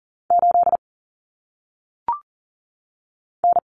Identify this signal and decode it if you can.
Morse code, heard as a keyed tone, about 40 wpm, 700 Hz. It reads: 8 N
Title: Common Cut Numbers - 1-Digit 40wpm